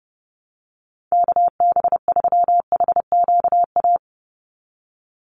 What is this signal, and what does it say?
Morse code, heard as a keyed tone, about 30 wpm, 700 Hz. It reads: KB3HQA
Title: USA Call Signs 30wpm